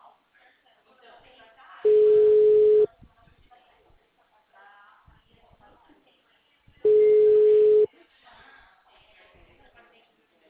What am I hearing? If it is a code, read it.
Ringback tone (Europe, 425 Hz), with speech.
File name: out- -2003- - - .30